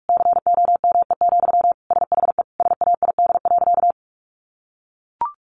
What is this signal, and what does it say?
Morse code, heard as a keyed tone, about 44 wpm, 700 Hz. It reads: COME, SHE SAID.
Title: Sentences from Top 100 Words 45wpm